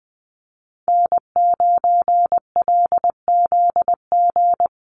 Morse code, heard as a keyed tone, about 20 wpm, 700 Hz. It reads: N9LZG
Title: USA Call Signs 20wpm